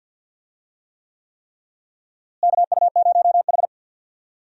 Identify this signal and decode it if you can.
Morse code, heard as a keyed tone, about 47 wpm, 700 Hz. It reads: KU0H
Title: USA Call Signs - 4 Characters - Repeated 3x - 50wpm